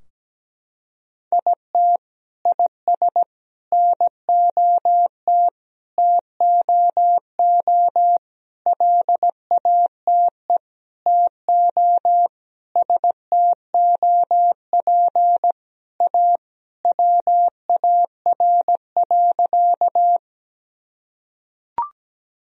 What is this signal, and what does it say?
Morse code, heard as a keyed tone, about 17 wpm, 700 Hz. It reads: IT IS NOT TOO LATE TO STOP A WAR.
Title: Sentences from Top 1000 Words - Review Part 2 - 17wpm